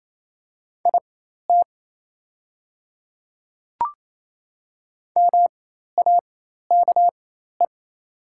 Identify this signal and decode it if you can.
Morse code, heard as a keyed tone, about 27 wpm, spaced as for 13 wpm, 700 Hz. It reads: IT MAKE
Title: 4x Character Spacing - Top 100 Words - 28wpm